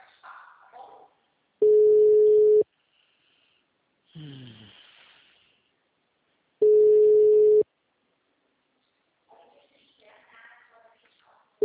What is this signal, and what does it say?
Ringback tone (Europe, 425 Hz), with speech.